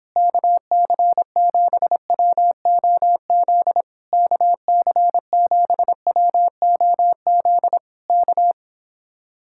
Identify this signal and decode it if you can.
Morse code, heard as a keyed tone, about 26 wpm, 700 Hz. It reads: KC7WOZ KC7WOZ K